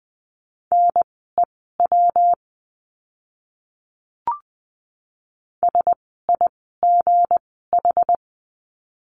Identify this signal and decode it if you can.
Morse code, heard as a keyed tone, about 20 wpm, spaced as for 15 wpm, 700 Hz. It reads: NEW SIGH
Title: 160 Lesson 26f - Letter g - Words 20wpm